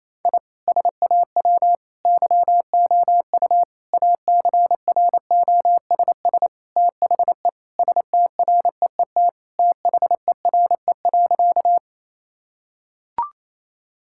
Morse code, heard as a keyed tone, about 28 wpm, 700 Hz. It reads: I SAW YOU ACROSS THE STREET THERE.
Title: Sentences from Top 1000 Words - Review Part 5 - 28wpm